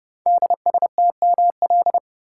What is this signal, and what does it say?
Morse code, heard as a keyed tone, about 30 wpm, 700 Hz. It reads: DSTML
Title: Code Groups - 5 Letters - 30wpm